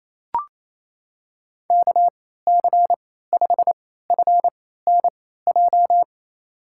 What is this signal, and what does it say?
Morse code, heard as a keyed tone, about 28 wpm, spaced as for 16 wpm, 700 Hz. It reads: KC5FNJ
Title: USA Call Signs - 6 Characters - 3x Character Spacing - 28wpm